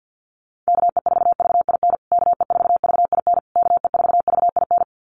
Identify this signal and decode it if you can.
Morse code, heard as a keyed tone, about 50 wpm, 700 Hz. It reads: KE4VIN KE4VIN KE4VIN
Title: USA Call Signs Repeated 3x 50wpm